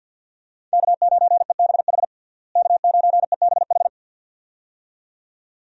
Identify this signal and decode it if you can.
Morse code, heard as a keyed tone, about 51 wpm, 700 Hz. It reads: K9EBH K9EBH
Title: USA Call Signs - 5 Characters - Repeated 2x - 50wpm